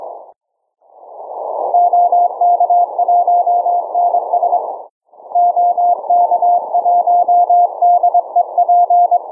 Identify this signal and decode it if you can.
Morse code, heard as a keyed tone, about 22 wpm, 700 Hz. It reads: OK1NS OK1DEP